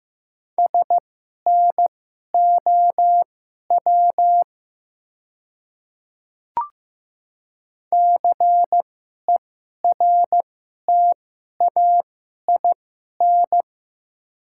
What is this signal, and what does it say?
Morse code, heard as a keyed tone, about 15 wpm, spaced as for 11 wpm, 700 Hz. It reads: SNOW CERTAIN
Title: 2x Character Spacing - Top 500 Words - 15wpm